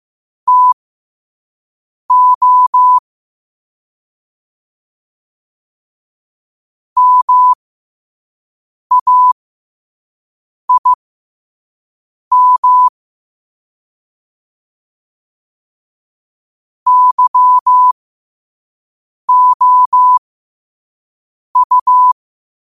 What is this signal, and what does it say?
Morse code, heard as a keyed tone, about 15 wpm, spaced as for 5 wpm, 1000 Hz. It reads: TO MAIM YOU
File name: qotdfiles / 5wpm